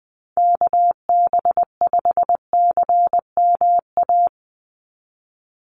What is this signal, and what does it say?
Morse code, heard as a keyed tone, about 20 wpm, 700 Hz. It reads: KB5CMA
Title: USA Call Signs 20wpm